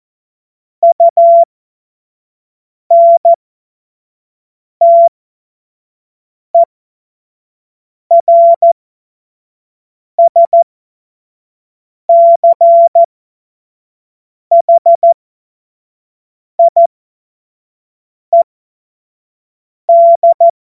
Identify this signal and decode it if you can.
Morse code, heard as a keyed tone, about 14 wpm, spaced as for 5 wpm, 680 Hz. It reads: UNTERSCHIED